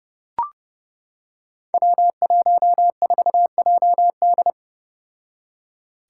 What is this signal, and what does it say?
Morse code, heard as a keyed tone, about 30 wpm, 700 Hz. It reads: W14JD
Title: Five Challenging Letter-Number Combinations 30wpm